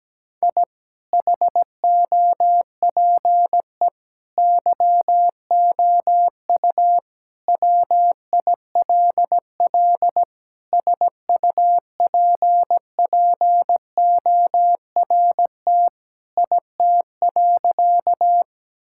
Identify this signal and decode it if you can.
Morse code, heard as a keyed tone, about 17 wpm, 700 Hz. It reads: I HOPE YOU WILL SUPPORT IT.
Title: Sentences from Top 1000 Words - Review Part 5 - 17wpm